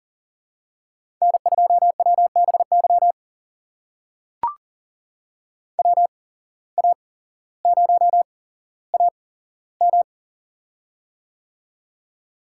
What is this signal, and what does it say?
Morse code, heard as a keyed tone, about 37 wpm, spaced as for 17 wpm, 700 Hz. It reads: N2WBY WA0AM
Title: USA Call Signs - 5 Characters - 8x Character Spacing - 40wpm